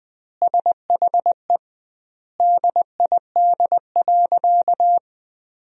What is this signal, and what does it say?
Morse code, heard as a keyed tone, about 20 wpm, 700 Hz. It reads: SHE DID.
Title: 2x Word-Spacing - Sentences from Top 100 Words - 20wpm